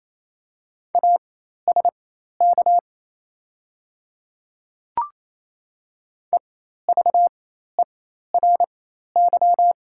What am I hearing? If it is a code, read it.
Morse code, heard as a keyed tone, about 28 wpm, spaced as for 13 wpm, 700 Hz. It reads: ASK EVERY
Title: 4x Character Spacing - Top 300 Words - 28wpm